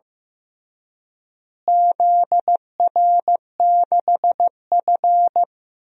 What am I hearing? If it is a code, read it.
Morse code, heard as a keyed tone, about 15 wpm, 700 Hz. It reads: ZR6F
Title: DXCC-100 Call Signs - 4 Characters - Repeated 2x - 15wpm